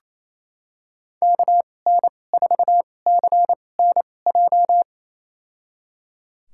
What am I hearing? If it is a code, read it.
Morse code, heard as a keyed tone, about 28 wpm, spaced as for 20 wpm, 690 Hz. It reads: KN4CNJ
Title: USA Call Signs - 6 Characters - 2x Character Spacing - 28wpm